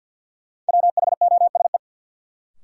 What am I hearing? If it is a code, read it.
Morse code, heard as a keyed tone, about 51 wpm, 700 Hz. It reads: WHOSE
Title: Sentences from Top 1200 Words - New Word Drill - 50wpm